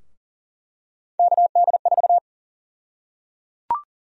Morse code, heard as a keyed tone, about 38 wpm, 700 Hz. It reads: KD4
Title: USA Call Sign Prefixes - Repeated 2x - 40wpm